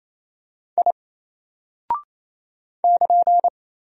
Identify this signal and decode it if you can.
Morse code, heard as a keyed tone, about 27 wpm, 700 Hz. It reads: I <KN>
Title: Single Character (Letter:Number:Punctuation:Prosign) 28wpm